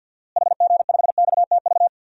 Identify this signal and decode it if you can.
Morse code, heard as a keyed tone, about 47 wpm, 700 Hz. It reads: HG5XTV
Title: DXCC-100 Call Signs - 6 Characters - Repeated 2x - 50wpm